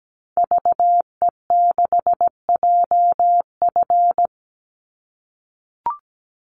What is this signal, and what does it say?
Morse code, heard as a keyed tone, about 17 wpm, 700 Hz. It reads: VE6JF
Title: Call Signs - 17wpm